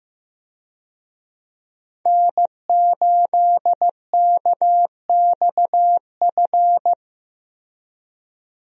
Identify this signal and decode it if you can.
Morse code, heard as a keyed tone, about 15 wpm, 700 Hz. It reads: N8KXF
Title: USA Call Signs - 5 Characters - Repeated 3x - 15wpm